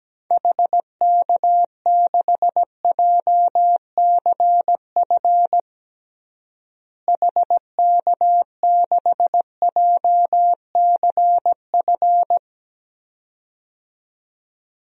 Morse code, heard as a keyed tone, about 17 wpm, 700 Hz. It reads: HK6JCF HK6JCF
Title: DXCC-100 Call Signs - 6 Characters - Repeated 2x - 17wpm